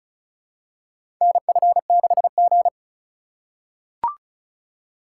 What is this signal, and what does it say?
Morse code, heard as a keyed tone, about 36 wpm, 690 Hz. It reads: NF6G
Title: USA Call Signs - 3x Character Spacing - 35wpm